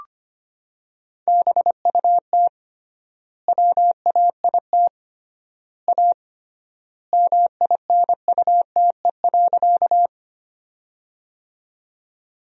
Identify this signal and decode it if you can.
Morse code, heard as a keyed tone, about 25 wpm, 700 Hz. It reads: BUT WAIT A MINUTE.
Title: 3x Word-Spacing - Sentences from Top 500 Words - 25wpm